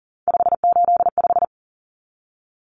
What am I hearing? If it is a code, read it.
Morse code, heard as a keyed tone, about 39 wpm, 700 Hz. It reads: 585
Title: Common Cut Numbers - 3-Digit 40wpm